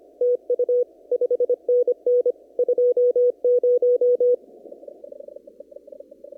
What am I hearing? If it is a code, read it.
Morse code, heard as a keyed tone, about 25 wpm, 490 Hz. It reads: TU 5NN 20
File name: CW / kt5j wpx cw-op-k5pi